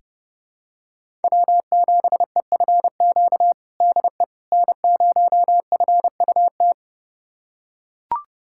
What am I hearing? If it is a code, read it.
Morse code, heard as a keyed tone, about 30 wpm, 700 Hz. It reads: W7EFQ DE N0FUT
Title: USA Callsign DE Callsign - 5 Characters - 30wpm